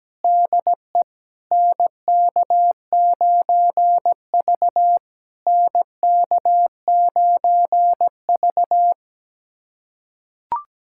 Morse code, heard as a keyed tone, about 20 wpm, 700 Hz. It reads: DE NK9V NK9V
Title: SOTA QSO Practice 17wpm